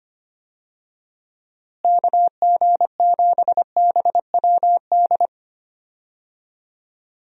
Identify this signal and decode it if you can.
Morse code, heard as a keyed tone, about 25 wpm, 700 Hz. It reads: KG7BWD
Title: USA Call Signs - 6 Characters - Repeated 3x - 25wpm